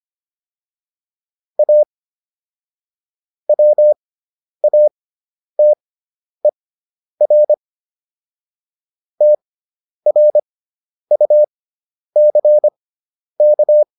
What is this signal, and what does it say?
Morse code, heard as a keyed tone, about 25 wpm, spaced as for 10 wpm, 600 Hz. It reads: A WATER TRUCK